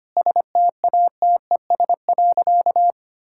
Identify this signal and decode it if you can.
Morse code, heard as a keyed tone, about 25 wpm, 700 Hz. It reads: STATES.
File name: practice / 25wpm / Sentences with 4-Word Phrases - 25wpm